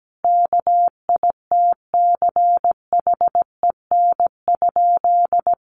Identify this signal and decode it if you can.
Morse code, heard as a keyed tone, about 17 wpm, 700 Hz. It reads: KITCHEN?
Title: Sentences from Top 1400 Words 17wpm